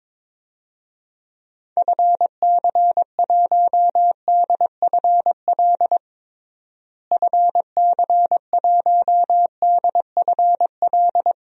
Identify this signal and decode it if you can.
Morse code, heard as a keyed tone, about 22 wpm, 700 Hz. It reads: FC1DFL FC1DFL
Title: DXCC-100 Call Signs - 6 Characters - Repeated 2x - 22wpm